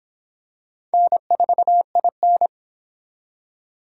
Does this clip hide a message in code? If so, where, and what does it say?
Yes: Morse code from 0.9 to 2.5 s, heard as a keyed tone, about 25 wpm, 700 Hz. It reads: N4IN